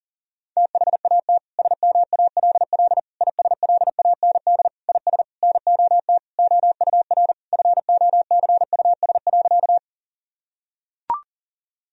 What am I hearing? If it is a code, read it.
Morse code, heard as a keyed tone, about 40 wpm, 700 Hz. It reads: THAT SMALL ISLAND IS NOT OUR FOCUS.
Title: Sentences from Top 1000 Words - Review Part 4 - 40wpm